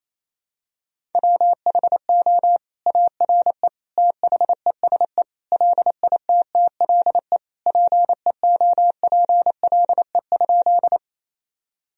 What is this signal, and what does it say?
Morse code, heard as a keyed tone, about 28 wpm, 700 Hz. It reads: WHO ARE THESE LITTLE PEOPLE?
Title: Sentences from Top 1000 Words - Review Part 1 - 28wpm